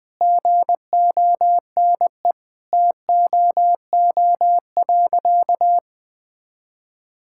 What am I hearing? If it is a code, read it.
Morse code, heard as a keyed tone, about 19 wpm, 700 Hz. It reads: GONE TOO.
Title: Sentences with 2-Word Phrases - Spoken Beginning - 20wpm